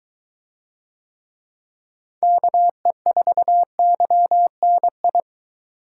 Morse code, heard as a keyed tone, about 22 wpm, 700 Hz. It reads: KE4YNI